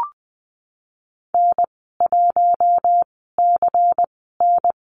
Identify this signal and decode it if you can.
Morse code, heard as a keyed tone, about 20 wpm, spaced as for 15 wpm, 700 Hz. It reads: N1CN
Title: 173 Lesson 28g - Number 9 - Call Signs 20wpm